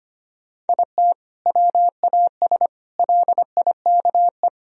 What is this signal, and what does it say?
Morse code, heard as a keyed tone, about 25 wpm, 700 Hz. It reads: IT WAS LIKE
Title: Sentences from Top 1000 Words - Review Part 1 - 25wpm